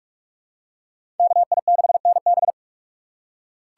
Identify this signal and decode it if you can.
Morse code, heard as a keyed tone, about 46 wpm, 700 Hz. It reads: KI6NB
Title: USA Call Signs - 5 Characters - 3x Character Spacing - 45wpm